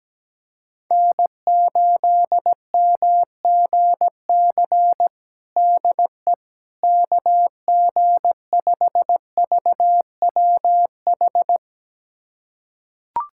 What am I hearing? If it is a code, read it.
Morse code, heard as a keyed tone, about 17 wpm, 700 Hz. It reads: N8MGC DE KG5VWH
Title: USA Callsign DE Callsign 17wpm